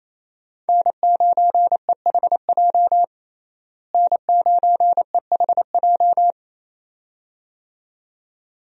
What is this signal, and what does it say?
Morse code, heard as a keyed tone, about 28 wpm, 700 Hz. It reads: N9EHJ N9EHJ
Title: USA Call Signs - 5 Characters - Repeated 2x - 28wpm